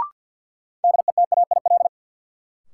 Morse code, heard as a keyed tone, about 53 wpm, 700 Hz. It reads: DETAIL
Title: Sentences from Top 1500 Words - New Word Drill - 50wpm